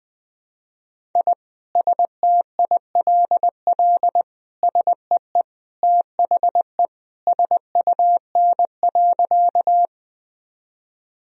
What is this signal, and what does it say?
Morse code, heard as a keyed tone, about 20 wpm, 700 Hz. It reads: I STILL SEE THE SUN.
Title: Sentences from Top 1000 Words - Review Part 1 - 20wpm